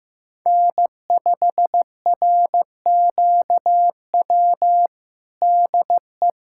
Morse code, heard as a keyed tone, about 15 wpm, 700 Hz. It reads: N5RQW DE
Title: USA Callsign DE Callsign 15wpm